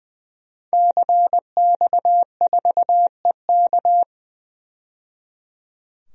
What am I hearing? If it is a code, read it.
Morse code, heard as a keyed tone, about 20 wpm, 700 Hz. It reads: CX4EK
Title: DXCC-100 Call Signs - 5 Characters - 20wpm